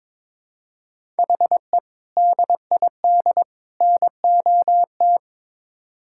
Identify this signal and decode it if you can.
Morse code, heard as a keyed tone, about 22 wpm, 700 Hz. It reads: HE DID NOT
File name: practice / 22wpm / Build A Phrase - 5-Word Phrases - 22wpm